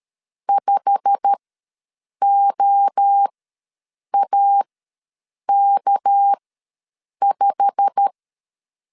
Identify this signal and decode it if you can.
Morse code, heard as a keyed tone, about 13 wpm, spaced as for 7 wpm, 800 Hz. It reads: 5OAK5
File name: LEZIONI / 8 YJ45 / 8 YJ45 ws 12wpm 3146